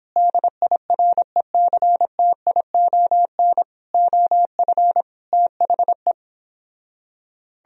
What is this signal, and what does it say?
Morse code, heard as a keyed tone, about 31 wpm, 700 Hz. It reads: DIRECTION OF THE